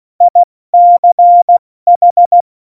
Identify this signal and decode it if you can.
Morse code, heard as a keyed tone, about 16 wpm, 700 Hz. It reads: ICH
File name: Morsekurs / Witze / Witz 18. Juni 2019, 70 F80